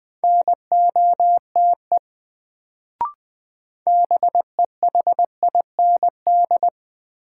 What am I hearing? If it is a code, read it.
Morse code, heard as a keyed tone, about 20 wpm, 700 Hz. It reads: NOTE BEHIND
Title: Sentences from Top 500 Words - New Word Drill - 20wpm